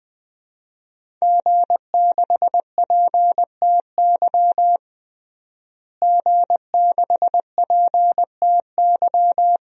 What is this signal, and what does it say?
Morse code, heard as a keyed tone, about 20 wpm, 700 Hz. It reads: G6PTY G6PTY
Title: DXCC-100 Call Signs - 5 Characters - Repeated 2x - 20wpm